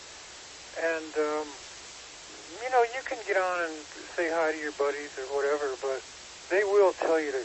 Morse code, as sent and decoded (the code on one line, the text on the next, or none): none